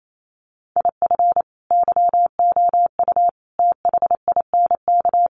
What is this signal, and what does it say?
Morse code, heard as a keyed tone, about 28 wpm, 700 Hz. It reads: IF YOU THINK
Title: Sets of 5 Words Encore2 28wpm